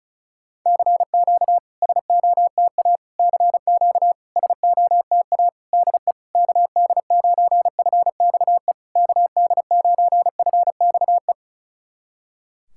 Morse code, heard as a keyed tone, about 36 wpm, 690 Hz. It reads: CQ SOTA CQ SOTA DE KD9FXE KD9FXE
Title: SOTA QSO Practice 35wpm